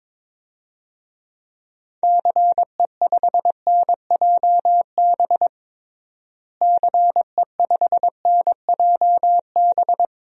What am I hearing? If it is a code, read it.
Morse code, heard as a keyed tone, about 22 wpm, 700 Hz. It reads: CE5NJB CE5NJB
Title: DXCC-100 Call Signs - 6 Characters - Repeated 3x - 22wpm